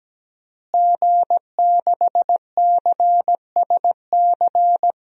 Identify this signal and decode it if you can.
Morse code, heard as a keyed tone, about 17 wpm, 700 Hz. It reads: G6CSC
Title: DXCC-100 Call Signs - 5 Characters - Repeated 3x - 17wpm